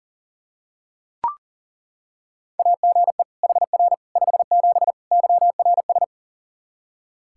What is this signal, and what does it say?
Morse code, heard as a keyed tone, about 40 wpm, 700 Hz. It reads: AGE HR 57 YRS
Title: Practice QSO - Exchange 3 - More Info - 40wpm